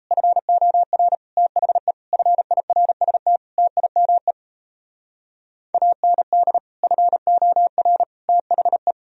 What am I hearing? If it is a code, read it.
Morse code, heard as a keyed tone, about 37 wpm, 700 Hz. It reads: FOR THE FIRST TIME AND FOR THE